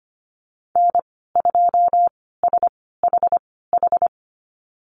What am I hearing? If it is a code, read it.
Morse code, heard as a keyed tone, about 25 wpm, spaced as for 16 wpm, 700 Hz. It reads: N2SHH
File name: Lesson 15g - Number 5 - Call Signs 25wpm